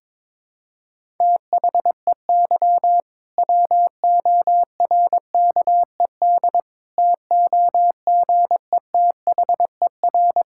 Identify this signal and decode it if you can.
Morse code, heard as a keyed tone, about 22 wpm, 700 Hz. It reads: THEY WORKED TOGETHER
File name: Difficult Sentences with Hints - 22wpm